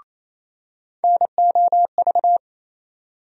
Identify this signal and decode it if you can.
Morse code, heard as a keyed tone, about 28 wpm, 700 Hz. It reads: NOV
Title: USA Call Sign Postfixes - 3 Letters 28wpm